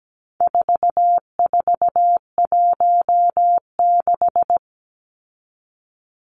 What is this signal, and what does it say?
Morse code, heard as a keyed tone, about 17 wpm, 700 Hz. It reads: 4416
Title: Numbers - 4-Digit 17wpm